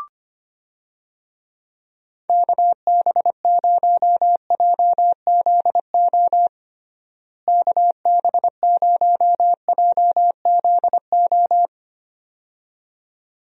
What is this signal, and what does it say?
Morse code, heard as a keyed tone, about 25 wpm, 700 Hz. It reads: KB0JZO KB0JZO
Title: USA Call Signs - 6 Characters - Repeated 2x - 25wpm